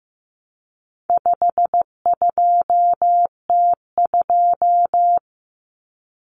Morse code, heard as a keyed tone, about 15 wpm, 700 Hz. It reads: 52T2
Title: Common Cut Numbers - 4-Digit 15wpm